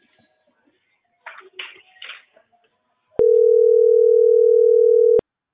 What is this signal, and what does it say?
Ringback tone (North America), 440 + 480 Hz.